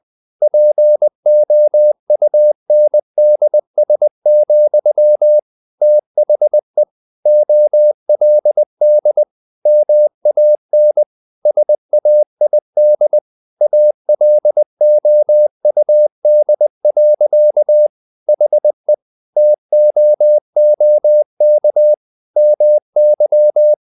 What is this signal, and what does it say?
Morse code, heard as a keyed tone, about 20 wpm, 600 Hz. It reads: POUNDS, THE OLD MAN SAID ALOUD. HE TOOK MY
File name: the old man and the sea part 2 20wpm